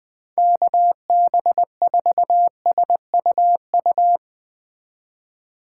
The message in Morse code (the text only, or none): KB4SUU